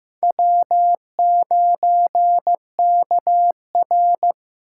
Morse code, heard as a keyed tone, about 18 wpm, 700 Hz. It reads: W9KR
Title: Speed Racing - USA 1x2 Call Signs 15wpm